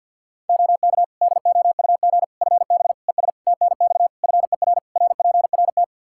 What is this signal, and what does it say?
Morse code, heard as a keyed tone, about 52 wpm, 700 Hz. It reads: OK DOUG FB ES TNX FER RPRT
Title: K1USN SST QSO Practice 50wpm